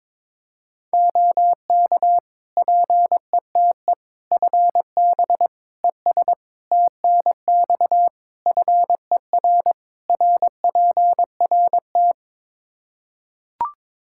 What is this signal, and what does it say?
Morse code, heard as a keyed tone, about 22 wpm, 700 Hz. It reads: OK PETE FB ES TNX FER RPRT
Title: Practice QSO - Exchange 2 - Info - 22wpm